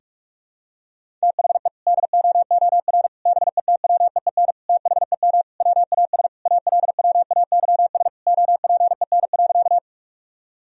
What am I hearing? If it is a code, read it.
Morse code, heard as a keyed tone, about 46 wpm, 700 Hz. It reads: THE DOOR BETWEEN THEM WAS ALWAYS OPEN.
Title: Sentences from Top 1000 Words - Review Part 1 - 45wpm